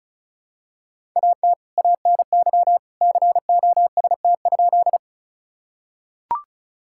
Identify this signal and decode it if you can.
Morse code, heard as a keyed tone, about 36 wpm, 700 Hz. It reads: AT ANY COST?
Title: Sentences from Top 1000 Words - Review Part 5 - 35wpm